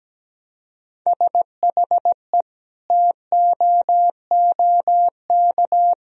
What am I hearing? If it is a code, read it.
Morse code, heard as a keyed tone, about 17 wpm, 700 Hz. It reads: SHE TOOK
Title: Sentences from Top 1000 Words - Review Part 1 - 17wpm